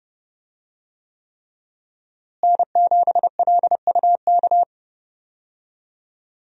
Morse code, heard as a keyed tone, about 30 wpm, 700 Hz. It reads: N7LUK
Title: USA Call Signs - 5 Characters - Repeated 3x - 30wpm